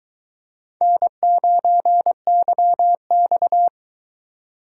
Morse code, heard as a keyed tone, about 22 wpm, 700 Hz. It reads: N9YX